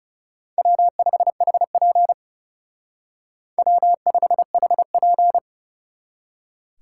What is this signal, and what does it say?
Morse code, heard as a keyed tone, about 37 wpm, 700 Hz. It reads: W5HP W5HP
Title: Speed Racing - USA Call Signs 30wpm